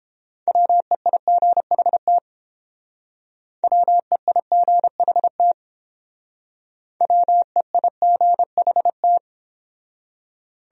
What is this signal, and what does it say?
Morse code, heard as a keyed tone, about 30 wpm, 700 Hz. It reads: WEIGHT WEIGHT WEIGHT